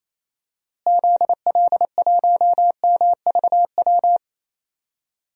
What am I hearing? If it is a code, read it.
Morse code, heard as a keyed tone, about 28 wpm, 700 Hz. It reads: ZL1MVW
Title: DXCC-100 Call Signs - 6 Characters - Repeated 3x - 28wpm